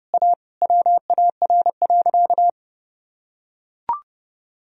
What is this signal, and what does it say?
Morse code, heard as a keyed tone, about 30 wpm, 700 Hz. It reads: A WAR.
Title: Sentences from Top 1000 Words - Review Part 2 - 30wpm